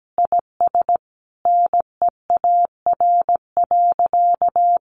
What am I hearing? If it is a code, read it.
Morse code, heard as a keyed tone, about 17 wpm, 700 Hz. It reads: IS NEAR.
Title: Sentences from Top 1500 Words 17wpm